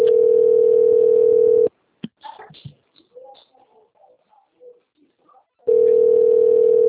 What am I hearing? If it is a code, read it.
Ringback tone (North America), 440 + 480 Hz.